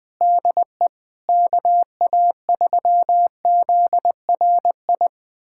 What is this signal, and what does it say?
Morse code, heard as a keyed tone, about 20 wpm, 700 Hz. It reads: DE KA3ZRI
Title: USA Callsign DE Callsign - 6 Characters - 20wpm